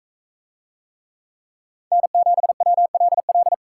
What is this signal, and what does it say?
Morse code, heard as a keyed tone, about 41 wpm, 700 Hz. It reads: N7WLL